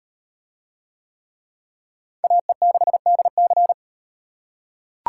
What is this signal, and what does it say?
Morse code, heard as a keyed tone, about 34 wpm, 700 Hz. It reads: AE6DC